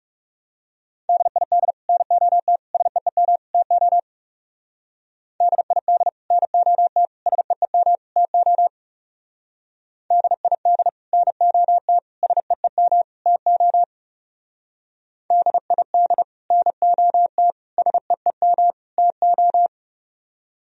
Morse code, heard as a keyed tone, about 34 wpm, 700 Hz. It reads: DID NOT SEEM TO DID NOT SEEM TO DID NOT SEEM TO DID NOT SEEM TO
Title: Speed-Racing - Sets of 4 Words 30wpm